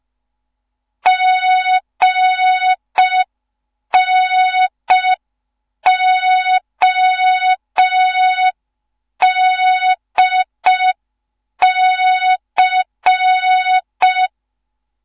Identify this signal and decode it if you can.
Morse code, heard as a keyed tone, about 5 wpm, 3700 Hz. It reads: GNODC
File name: Morze-HA2MN / hang / 15 GYAKORLAT